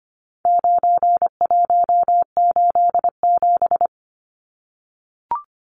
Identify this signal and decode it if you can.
Morse code, heard as a keyed tone, about 25 wpm, 700 Hz. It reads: 9187
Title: Numbers - 4-Digit 25wpm